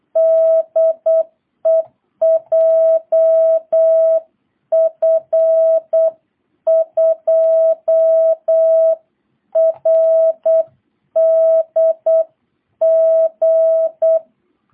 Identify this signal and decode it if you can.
Morse code, heard as a keyed tone, about 8 wpm, 660 Hz. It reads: DE JF2RDG